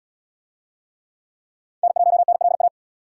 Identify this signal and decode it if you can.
Morse code, heard as a keyed tone, about 75 wpm, 700 Hz. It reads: N1NLA